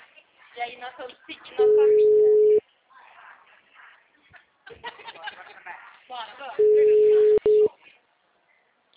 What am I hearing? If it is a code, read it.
Ringback tone (Europe, 425 Hz), with speech.
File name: out- -2032- - - .61623